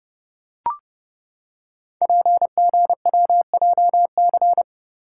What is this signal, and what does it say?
Morse code, heard as a keyed tone, about 30 wpm, 700 Hz. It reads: PGWJC
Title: Code Groups - 5 Letters - 30wpm